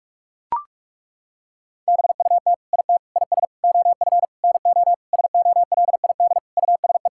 Morse code, heard as a keyed tone, about 41 wpm, 700 Hz. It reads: BUT IT IS OF NO SOLID USE
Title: Sentences from Top 1600 Words 45wpm